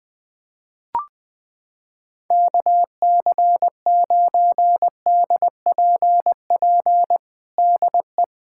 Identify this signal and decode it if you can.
Morse code, heard as a keyed tone, about 20 wpm, 700 Hz. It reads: KC9DPP DE
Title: USA Callsign DE Callsign - 6 Characters - 20wpm